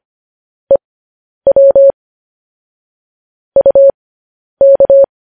Morse code, heard as a keyed tone, about 25 wpm, spaced as for 10 wpm, 550 Hz. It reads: EW UK
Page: news headlines in morse code at 10 wpm